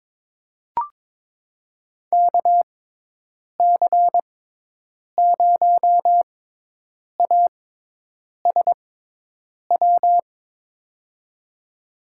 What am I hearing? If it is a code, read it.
Morse code, heard as a keyed tone, about 22 wpm, spaced as for 8 wpm, 700 Hz. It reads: KC0ASW
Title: USA Call Signs - 6 Characters - 6x Character Spacing - 22wpm